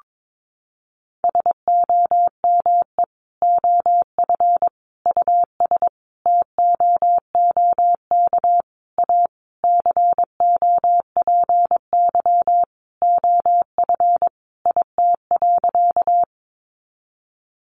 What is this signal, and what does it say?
Morse code, heard as a keyed tone, about 22 wpm, 700 Hz. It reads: SOME OF US TOOK A COPY OF IT.
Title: Sentences from Top 1400 Words 22wpm